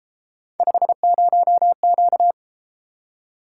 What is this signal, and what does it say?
Morse code, heard as a keyed tone, about 32 wpm, 700 Hz. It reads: 50Q